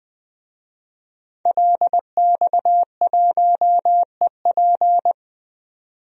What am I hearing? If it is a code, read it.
Morse code, heard as a keyed tone, about 20 wpm, 700 Hz. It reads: LX1EP
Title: DXCC-100 Call Signs - 5 Characters - 20wpm